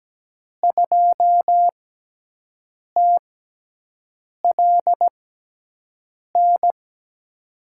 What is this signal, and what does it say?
Morse code, heard as a keyed tone, about 17 wpm, spaced as for 6 wpm, 700 Hz. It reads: 2TLN
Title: USA Call Signs - 5 Characters - 6x Character Spacing - 17wpm